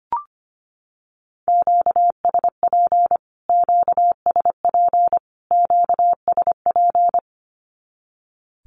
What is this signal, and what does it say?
Morse code, heard as a keyed tone, about 25 wpm, 700 Hz. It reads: QSP QSP QSP
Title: Q-Codes Repeated 3x 25wpm